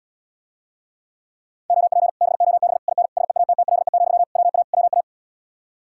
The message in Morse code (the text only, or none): OP BOB IN SEATTLE, WA WA